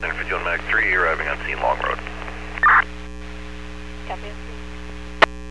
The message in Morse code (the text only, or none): none